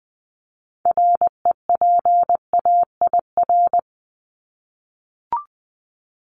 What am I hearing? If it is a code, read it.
Morse code, heard as a keyed tone, about 20 wpm, 700 Hz. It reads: REPAIR
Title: Top 1501-1750 Words 20wpm